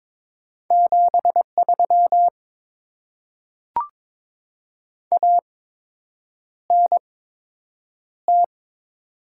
Morse code, heard as a keyed tone, about 22 wpm, spaced as for 8 wpm, 700 Hz. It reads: 73 ANT
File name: Top 20 QSO Elements - 8x Character Spacing - 22wpm